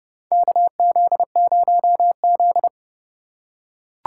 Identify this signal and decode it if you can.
Morse code, heard as a keyed tone, about 32 wpm, 700 Hz. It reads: KZ0Z
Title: USA Call Signs - 4 Characters - 2x Character Spacing - 30wpm